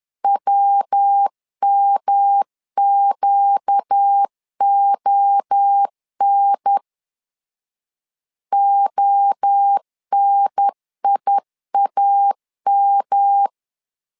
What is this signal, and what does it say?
Morse code, heard as a keyed tone, about 10 wpm, 800 Hz. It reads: WMQON ONIAM
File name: LEZIONI / 4 Q1 10wpm 2448